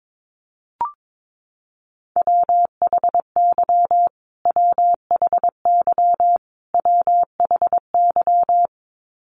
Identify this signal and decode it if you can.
Morse code, heard as a keyed tone, about 22 wpm, 700 Hz. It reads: WHY WHY WHY
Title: Top 300 Words Repeated 22wpm